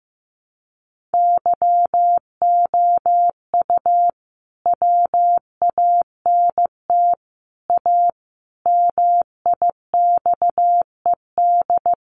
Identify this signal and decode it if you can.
Morse code, heard as a keyed tone, about 14 wpm, 700 Hz. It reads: YOU WANT A MIXED
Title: Sentences from Top 1500 Words 15wpm